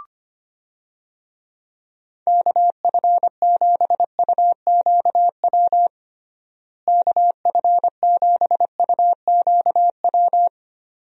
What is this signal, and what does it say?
Morse code, heard as a keyed tone, about 25 wpm, 700 Hz. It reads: KF7UQW KF7UQW
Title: USA Call Signs - 6 Characters - Repeated 2x - 25wpm